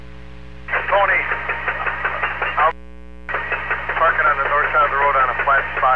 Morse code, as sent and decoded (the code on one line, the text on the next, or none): none